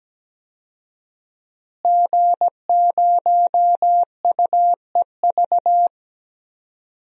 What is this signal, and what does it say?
Morse code, heard as a keyed tone, about 17 wpm, 700 Hz. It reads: G0UEV
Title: DXCC-100 Call Signs - 5 Characters - Repeated 3x - 17wpm